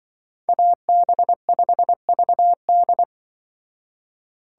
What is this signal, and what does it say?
Morse code, heard as a keyed tone, about 24 wpm, 700 Hz. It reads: AB5VD
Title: Speed Racing - USA 2x2 Call Signs 20wpm